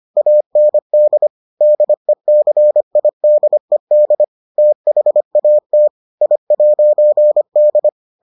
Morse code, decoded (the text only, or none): AND DECIDED THAT I'D